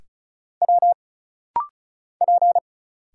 Morse code, heard as a keyed tone, about 36 wpm, 700 Hz. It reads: W P
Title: Single Letters - Rapid-Fire 35wpm